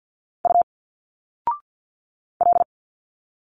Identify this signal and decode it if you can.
Morse code, heard as a keyed tone, about 50 wpm, 700 Hz. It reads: U L
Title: Single Letters 50wpm